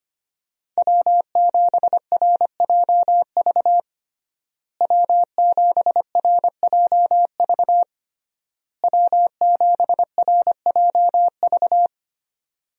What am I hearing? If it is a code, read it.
Morse code, heard as a keyed tone, about 25 wpm, 700 Hz. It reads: W7RJV W7RJV W7RJV
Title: USA Call Signs - 5 Characters - Repeated 3x - 25wpm